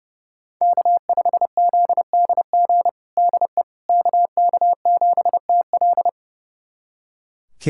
Morse code, heard as a keyed tone, about 30 wpm, 700 Hz. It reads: K5ZDG DE KK7TL
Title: USA Callsign DE Callsign - 5 Characters - 30wpm